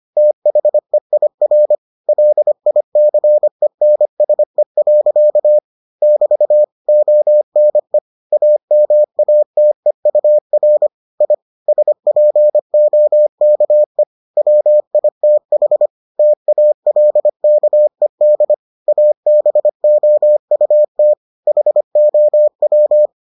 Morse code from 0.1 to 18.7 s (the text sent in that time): THEIR LICENSE. <BT> ONE AMATEUR I SPOKE WITH TALKED